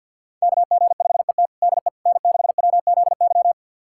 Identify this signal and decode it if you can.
Morse code, heard as a keyed tone, about 45 wpm, 700 Hz. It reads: KG5ET DE N6WZY
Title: USA Callsign DE Callsign - 5 Characters - 50wpm